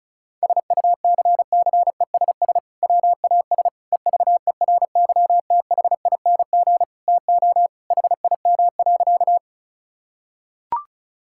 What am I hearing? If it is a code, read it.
Morse code, heard as a keyed tone, about 36 wpm, 700 Hz. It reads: SUCCESS WAS EVERYTHING TO HIM.
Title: Sentences from Top 1000 Words - Review Part 6 - 35wpm